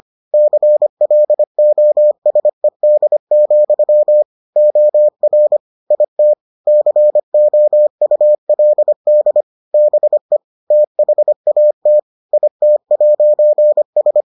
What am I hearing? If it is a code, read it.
Morse code, heard as a keyed tone, about 25 wpm, 600 Hz. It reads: CLOSED, OR IT COULD BE THAT IT'S